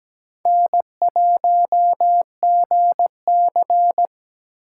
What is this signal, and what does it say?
Morse code, heard as a keyed tone, about 21 wpm, 700 Hz. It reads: N1GC